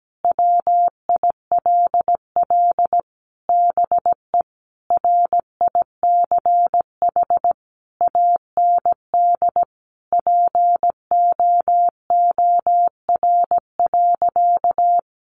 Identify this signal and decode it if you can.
Morse code, heard as a keyed tone, about 17 wpm, 700 Hz. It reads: WILL BE RICH AND POOR.
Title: Sentences from Top 1100 Words 17wpm